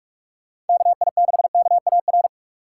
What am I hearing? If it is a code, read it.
Morse code, heard as a keyed tone, about 42 wpm, 700 Hz. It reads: KI6KAR